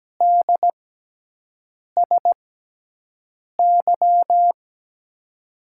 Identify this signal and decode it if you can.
Morse code, heard as a keyed tone, about 17 wpm, spaced as for 6 wpm, 700 Hz. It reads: DSY
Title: USA Call Signs - 6x Character Spacing - 17wpm